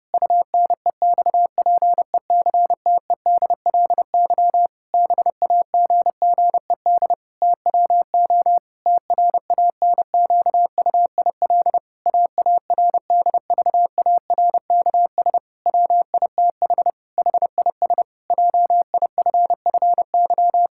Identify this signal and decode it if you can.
Morse code, heard as a keyed tone, about 30 wpm, 700 Hz. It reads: UNEXPECTEDLY BAGGED TWO TRANQUIL AARDVARKS WITH HIS JIFFY
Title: Pangram Challenge v2 - 30wpm